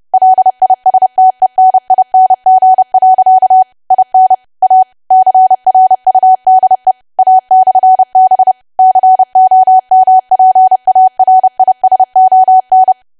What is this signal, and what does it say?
Morse code, heard as a keyed tone, about 30 wpm, 750 Hz. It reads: LISTENING. IN A CRUDE A/B COMPARISON
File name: cwa / W1AW-Level3 / 30WPM